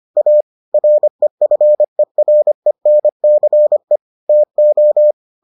Morse code, heard as a keyed tone, about 25 wpm, 600 Hz. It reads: A REFERENCE TO